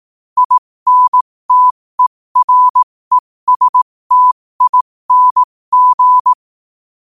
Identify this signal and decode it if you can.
Morse code, heard as a keyed tone, about 18 wpm, spaced as for 15 wpm, 1000 Hz. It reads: INTERESTING
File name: qotdfiles / 15wpm